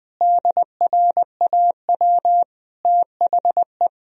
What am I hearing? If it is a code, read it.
Morse code, heard as a keyed tone, about 20 wpm, 700 Hz. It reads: DRAW THE
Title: Sentences from Top 1000 Words - Review Part 1 - 20wpm